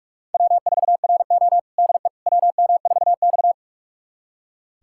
Morse code, heard as a keyed tone, about 47 wpm, 700 Hz. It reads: W4RO DE WM4X
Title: USA Callsign DE Callsign - 4 Characters - 45wpm